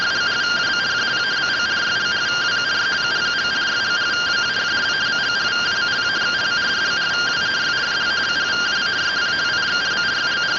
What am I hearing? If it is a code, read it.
Two-tone teleprinter signal, about 170 Hz shift.